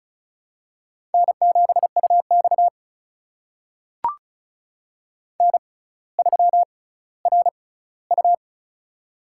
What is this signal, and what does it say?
Morse code, heard as a keyed tone, about 36 wpm, spaced as for 18 wpm, 700 Hz. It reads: N7UX N3RU
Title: USA Call Signs - 4 Characters - 6x Character Spacing - 35wpm